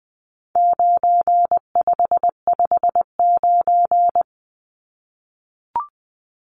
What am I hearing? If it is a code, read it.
Morse code, heard as a keyed tone, about 19 wpm, 700 Hz. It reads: 9559
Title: Numbers - 4-Digit 20wpm